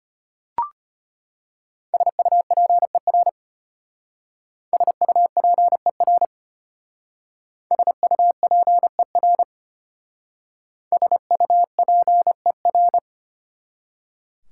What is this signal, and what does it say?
Morse code, heard as a keyed tone, about 29 wpm, 700 Hz. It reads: SUPER SUPER SUPER SUPER
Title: Speed-Racing Prefixes and Suffixes 25wpm